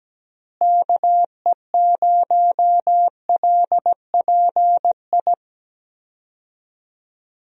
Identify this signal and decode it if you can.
Morse code, heard as a keyed tone, about 17 wpm, 700 Hz. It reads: KE0LPI
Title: USA Call Signs - 6 Characters - Repeated 3x - 17wpm